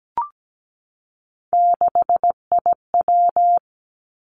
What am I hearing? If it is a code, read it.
Morse code, heard as a keyed tone, about 17 wpm, 700 Hz. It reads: 6IW
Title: Three Challenging Letter-Number Combinations 17wpm